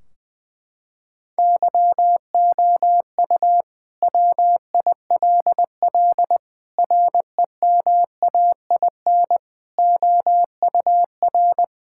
Morse code, heard as a keyed tone, about 21 wpm, 700 Hz. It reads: YOU WILL REMAIN OUR
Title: Sentences from Top 1600 Words 20wpm